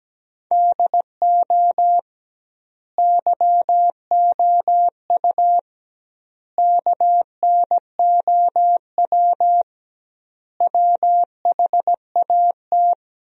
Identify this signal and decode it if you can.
Morse code, heard as a keyed tone, about 17 wpm, 700 Hz. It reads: DO YOU KNOW WHAT
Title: 2x Word-Spacing - Sentences from Top 100 Words - 17wpm